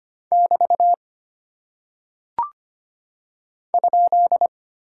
Morse code, heard as a keyed tone, about 25 wpm, 700 Hz. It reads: <BT> ?
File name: Single Punctuation-Prosign - 25wpm